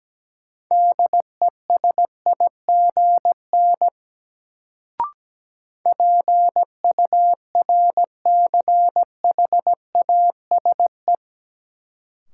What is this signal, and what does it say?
Morse code, heard as a keyed tone, about 17 wpm, 700 Hz. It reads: DESIGN PURCHASE
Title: Sentences from Top 800 Words - New Word Drill - 17wpm